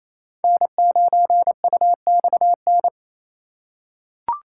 Morse code, heard as a keyed tone, about 28 wpm, 700 Hz. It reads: N9UXN
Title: USA Call Signs - 3x Character Spacing - 28wpm